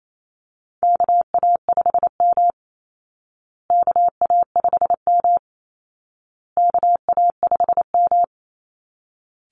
Morse code, heard as a keyed tone, about 28 wpm, 700 Hz. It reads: KA5M KA5M KA5M
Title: Call Signs Repeated 2x - 28wpm